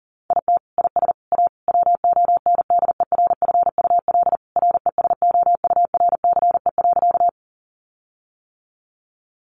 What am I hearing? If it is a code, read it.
Morse code, heard as a keyed tone, about 40 wpm, 700 Hz. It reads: IT IS A WONDERFUL RESOURCE.
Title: Sentences from Top 1300 Words 40wpm